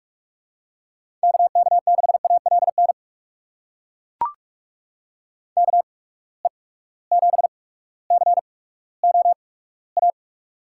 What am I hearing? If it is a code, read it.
Morse code, heard as a keyed tone, about 42 wpm, spaced as for 19 wpm, 700 Hz. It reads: KK6ALN KE7COA
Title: USA Call Signs - 6 Characters - 8x Character Spacing - 45wpm